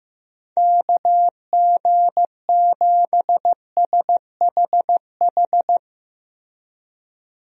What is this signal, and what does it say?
Morse code, heard as a keyed tone, about 15 wpm, 700 Hz. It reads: KG7SHH
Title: USA Call Signs - 6 Characters - Repeated 3x - 15wpm